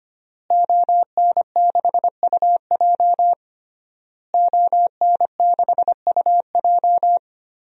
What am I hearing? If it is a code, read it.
Morse code, heard as a keyed tone, about 25 wpm, 700 Hz. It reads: ON6UJ ON6UJ
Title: DXCC-100 Call Signs - 5 Characters - Repeated 3x - 25wpm